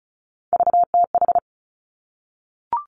Morse code, heard as a keyed tone, about 36 wpm, 700 Hz. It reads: VTH
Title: Three Challenging Letter-Number Combinations 35wpm